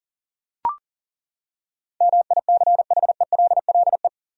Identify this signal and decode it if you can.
Morse code, heard as a keyed tone, about 40 wpm, 700 Hz. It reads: MICHELLE
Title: USA Common Names - 40wpm